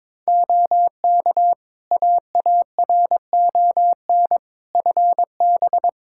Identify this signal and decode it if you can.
Morse code, heard as a keyed tone, about 21 wpm, 700 Hz. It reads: OK AARON FB
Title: Practice QSO - Exchange 2 - Info - 22wpm